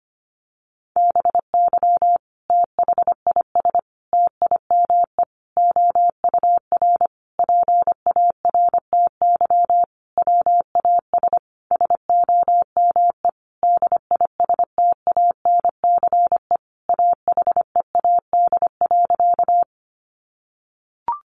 Morse code, heard as a keyed tone, about 25 wpm, 700 Hz. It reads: BY THIS TIME OUR PARTY WAS SOME DISTANCE AHEAD.
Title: Sentences from Top 1400 Words 25wpm